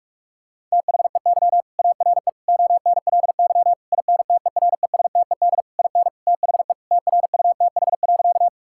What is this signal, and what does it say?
Morse code, heard as a keyed tone, about 46 wpm, 700 Hz. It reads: THEY ARE ONLY INTERESTED IN THE TRUTH.
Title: Sentences from Top 1000 Words - Review Part 6 - 45wpm